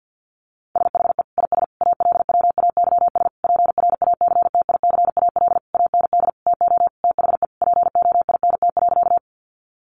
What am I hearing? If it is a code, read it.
Morse code, heard as a keyed tone, about 50 wpm, 700 Hz. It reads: SHE IS ALWAYS PRACTICAL AND TO THE POINT.
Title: Sentences from Top 1400 Words 50wpm